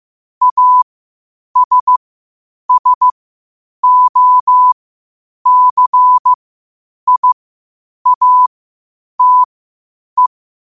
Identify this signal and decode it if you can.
Morse code, heard as a keyed tone, about 15 wpm, spaced as for 8 wpm, 1000 Hz. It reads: ASSOCIATE
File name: qotdfiles / 8wpm